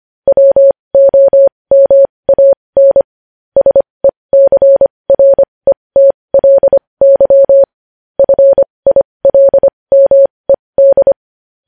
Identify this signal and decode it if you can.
Morse code, heard as a keyed tone, about 25 wpm, spaced as for 20 wpm, 550 Hz. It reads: WOMAN SECRETLY FILMED
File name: podcast / News headlines-1x-25x20